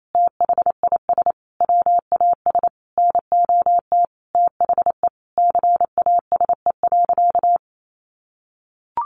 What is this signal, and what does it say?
Morse code, heard as a keyed tone, about 28 wpm, 700 Hz. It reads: THIS WAS NOT THE CASE.
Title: Sentences from Top 1200 Words 28wpm